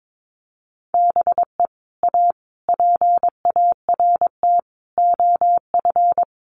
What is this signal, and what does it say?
Morse code, heard as a keyed tone, about 22 wpm, 700 Hz. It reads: BE A PART OF
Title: Sets of 5 Words Encore 22wpm